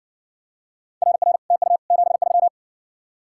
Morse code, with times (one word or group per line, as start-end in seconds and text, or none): 1.0-1.4 RR
1.5-1.8 TU
1.9-2.5 73